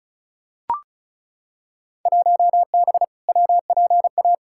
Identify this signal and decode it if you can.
Morse code, heard as a keyed tone, about 35 wpm, 690 Hz. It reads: 1B WPA
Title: Field Day Exchanges Encore - 35wpm